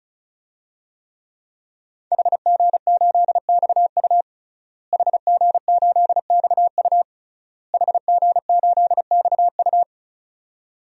Morse code, heard as a keyed tone, about 36 wpm, 700 Hz. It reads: HG8XU HG8XU HG8XU
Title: DXCC-100 Call Signs - 5 Characters - Repeated 3x - 35wpm